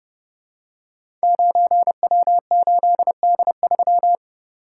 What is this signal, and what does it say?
Morse code, heard as a keyed tone, about 30 wpm, 700 Hz. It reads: 9W8D3
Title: Code Groups - 5 Letter:Numbers - 30wpm